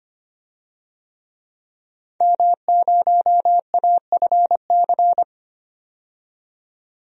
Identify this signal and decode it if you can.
Morse code, heard as a keyed tone, about 25 wpm, 700 Hz. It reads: M0AFC
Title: DXCC-100 Call Signs - 5 Characters - 25wpm